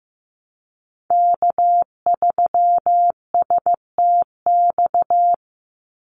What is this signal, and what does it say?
Morse code, heard as a keyed tone, about 15 wpm, 700 Hz. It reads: K3STX
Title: Call Signs Repeated 3x - 15wpm